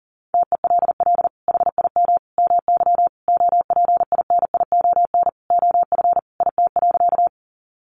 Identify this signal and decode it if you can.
Morse code, heard as a keyed tone, about 40 wpm, 700 Hz. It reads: TELL HIM MY OPINION OF IT.
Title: Sentences from Top 1200 Words 40wpm